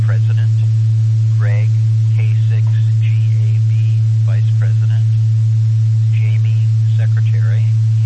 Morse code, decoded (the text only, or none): none